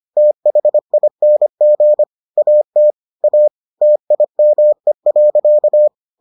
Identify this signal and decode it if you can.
Morse code, heard as a keyed tone, about 25 wpm, 600 Hz. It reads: THING AT A TIME.